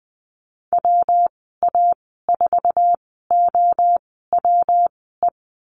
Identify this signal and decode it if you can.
Morse code, heard as a keyed tone, about 19 wpm, spaced as for 14 wpm, 700 Hz. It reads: WA4OWE
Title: 128 Lesson 21g - Number 6 - Call Signs 20wpm